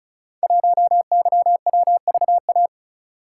Morse code, heard as a keyed tone, about 36 wpm, 700 Hz. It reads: 1YWVA
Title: Code Groups - 5 Letter:Numbers - 35wpm